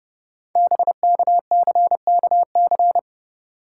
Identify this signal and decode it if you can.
Morse code, heard as a keyed tone, about 30 wpm, 700 Hz. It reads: BKCKC
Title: Code Groups - 5 Letters - 30wpm